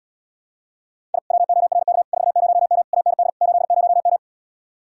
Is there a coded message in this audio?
Yes: Morse code from 1.1 to 4.2 s, heard as a keyed tone, about 80 wpm, 700 Hz. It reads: I WORK 40M AND 80M